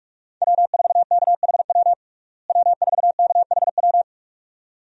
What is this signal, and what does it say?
Morse code, heard as a keyed tone, about 46 wpm, 700 Hz. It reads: W4KHW W4KHW
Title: USA Call Signs - 5 Characters - Repeated 2x - 45wpm